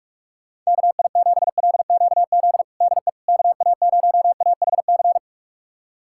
Morse code, heard as a keyed tone, about 46 wpm, 700 Hz. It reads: KI7LQZ DE KA0AHC
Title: USA Callsign DE Callsign 45wpm